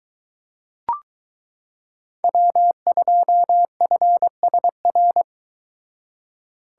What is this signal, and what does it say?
Morse code, heard as a keyed tone, about 22 wpm, 700 Hz. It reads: W2FSR